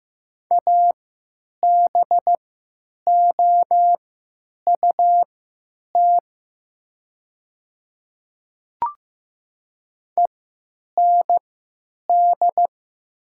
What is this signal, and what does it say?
Morse code, heard as a keyed tone, about 15 wpm, spaced as for 9 wpm, 700 Hz. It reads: ABOUT END
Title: 3x Character Spacing - Top 500 Words - 15wpm